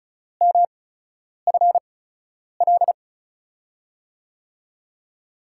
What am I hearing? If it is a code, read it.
Morse code, heard as a keyed tone, about 36 wpm, spaced as for 10 wpm, 700 Hz. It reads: MFL
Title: USA Call Signs - 5 Characters - 8x Character Spacing - 35wpm